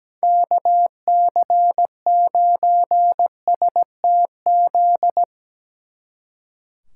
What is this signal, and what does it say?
Morse code, heard as a keyed tone, about 19 wpm, 700 Hz. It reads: KC9STZ
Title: Speed Racing - USA 2x3 Call Signs 17wpm